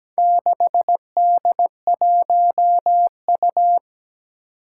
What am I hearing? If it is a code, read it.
Morse code, heard as a keyed tone, about 17 wpm, 700 Hz. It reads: 6D1U
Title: Four Challenging Letter-Number Combinations 17wpm